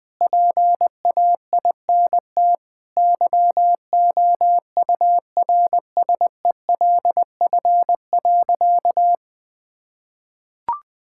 Morse code, heard as a keyed tone, about 20 wpm, 700 Hz. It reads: PAINT YOURSELF.
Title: Sentences from Top 1000 Words - Review Part 6 - 20wpm